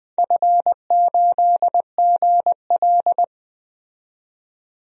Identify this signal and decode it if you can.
Morse code, heard as a keyed tone, about 20 wpm, 700 Hz. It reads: F8GL
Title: DXCC-100 Call Signs - 4 Characters - 20wpm